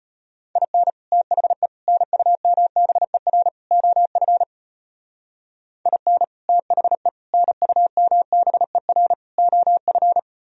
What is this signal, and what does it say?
Morse code, heard as a keyed tone, about 32 wpm, 700 Hz. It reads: IN THE NUMBER OF IN THE NUMBER OF